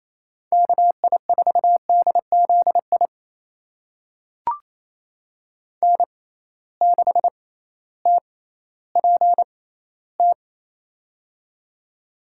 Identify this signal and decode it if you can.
Morse code, heard as a keyed tone, about 28 wpm, spaced as for 15 wpm, 700 Hz. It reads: KI4DZI N6TPT
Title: USA Call Signs - 6x Character Spacing - 28wpm